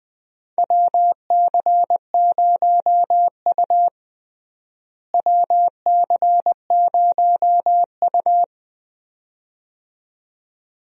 Morse code, heard as a keyed tone, about 20 wpm, 700 Hz. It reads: WC0U WC0U
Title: USA Call Signs - 4 Characters - Repeated 2x - 20wpm